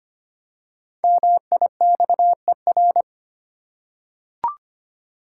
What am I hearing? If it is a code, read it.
Morse code, heard as a keyed tone, about 25 wpm, 690 Hz. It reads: MIXER
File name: practice / 25wpm / Easy Words with Hints - 25wpm